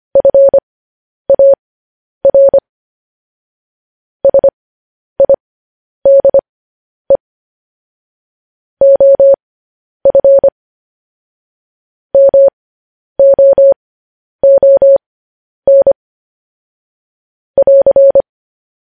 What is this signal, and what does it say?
Morse code, heard as a keyed tone, about 25 wpm, spaced as for 10 wpm, 550 Hz. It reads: FAR SIDE OF MOON <AR>